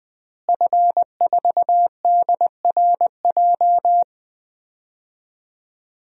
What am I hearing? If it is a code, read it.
Morse code, heard as a keyed tone, about 20 wpm, 700 Hz. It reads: F4DRJ
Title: DXCC-100 Call Signs - 5 Characters - Repeated 3x - 20wpm